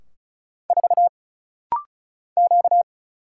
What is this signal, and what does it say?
Morse code, heard as a keyed tone, about 34 wpm, 700 Hz. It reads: 4 Q
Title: Single Letter-Number Alt - Rapid-Fire 35wpm